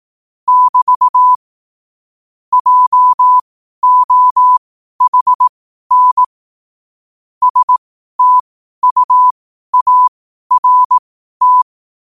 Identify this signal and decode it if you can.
Morse code, heard as a keyed tone, about 18 wpm, spaced as for 12 wpm, 1000 Hz. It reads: <BT> JOHN STUART
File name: qotdfiles / 12wpm